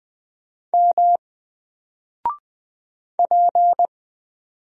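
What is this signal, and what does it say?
Morse code, heard as a keyed tone, about 20 wpm, 700 Hz. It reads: M P
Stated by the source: Single letter and number Morse drill at 20 wpm, from N4LCD.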